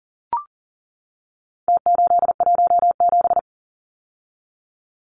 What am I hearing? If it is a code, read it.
Morse code, heard as a keyed tone, about 40 wpm, 700 Hz. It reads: T817
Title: Numbers - Contest Style - 4-Digit 40wpm